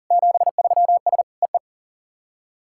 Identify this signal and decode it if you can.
Morse code, heard as a keyed tone, about 40 wpm, 700 Hz. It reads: 73S EE
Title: SOTA QSO Practice 40wpm